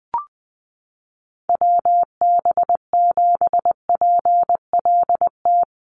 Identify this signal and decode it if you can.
Morse code, heard as a keyed tone, about 20 wpm, 700 Hz. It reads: WB7PLT
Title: USA Call Signs Repeated 3x 20wpm